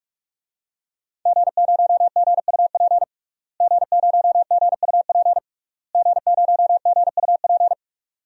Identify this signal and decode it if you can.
Morse code, heard as a keyed tone, about 42 wpm, 700 Hz. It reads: G0GUP G0GUP G0GUP
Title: DXCC-100 Call Signs - 5 Characters - Repeated 3x - 45wpm